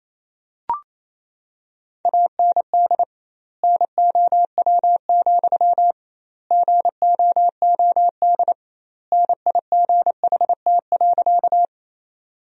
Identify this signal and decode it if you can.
Morse code, heard as a keyed tone, about 27 wpm, 700 Hz. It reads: AND NOW, GOOD NIGHT.
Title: 2x Word-Spacing - Sentences from Top 300 Words - 28wpm